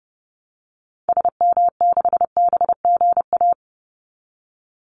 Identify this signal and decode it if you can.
Morse code, heard as a keyed tone, about 30 wpm, 700 Hz. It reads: SM6BGA
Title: Call Signs Repeated 2x - 30wpm